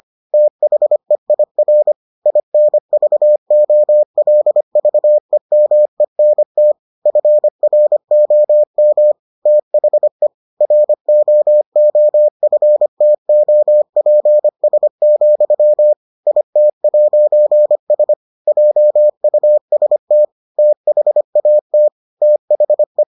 Morse code, heard as a keyed tone, about 25 wpm, 600 Hz. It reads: THEIR INVOLVEMENT FROM THE ROOFTOPS, IT'S JUST THAT THE